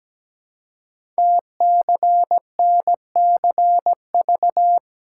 Morse code, heard as a keyed tone, about 17 wpm, 700 Hz. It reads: TCNCV
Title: Code Groups - 5 Letters - 17wpm